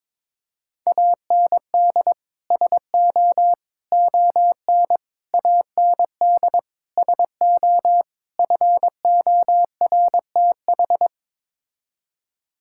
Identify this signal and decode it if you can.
Morse code, heard as a keyed tone, about 22 wpm, 700 Hz. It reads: AND SO ON AND SO FORTH
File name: practice / 22wpm / Sets of 6 Words - 22wpm